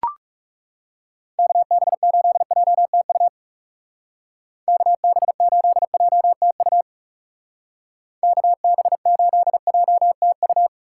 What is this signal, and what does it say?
Morse code, heard as a keyed tone, about 36 wpm, 700 Hz. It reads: KB8JTU KB8JTU KB8JTU
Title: Speed Racing - USA 2x3 Call Signs 30wpm